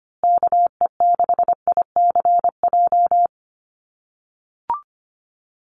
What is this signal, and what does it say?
Morse code, heard as a keyed tone, about 25 wpm, 700 Hz. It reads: KE6ICJ
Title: USA Call Signs Repeated 2x 25wpm